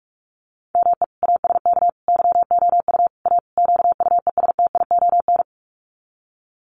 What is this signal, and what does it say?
Morse code, heard as a keyed tone, about 44 wpm, 700 Hz. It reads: ME ASK YOU A QUESTION
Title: Sets of 5 Words Encore2 45wpm